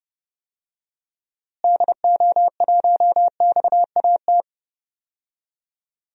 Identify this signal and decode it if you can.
Morse code, heard as a keyed tone, about 30 wpm, 700 Hz. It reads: DO1XAT
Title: DXCC-100 Call Signs - 6 Characters - Repeated 2x - 30wpm